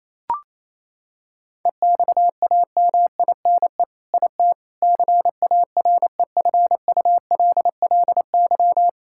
Morse code, heard as a keyed tone, about 28 wpm, 700 Hz. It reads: EXAMINE IT CAREFULLY
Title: Sentences from Top 1600 Words 28wpm